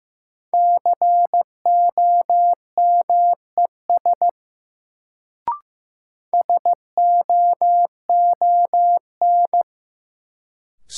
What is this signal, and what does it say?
Morse code, heard as a keyed tone, about 15 wpm, 700 Hz. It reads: COMES SOON
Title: Sets of 3 Words - New Word Drill - 15wpm